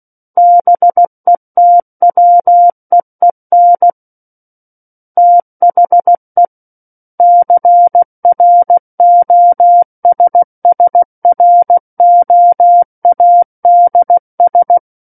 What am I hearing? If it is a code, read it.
Morse code, heard as a keyed tone, about 16 wpm, 700 Hz. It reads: BETWEEN THE CROSSROADS
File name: Book One / 07-Chapter 7 - 16 WPM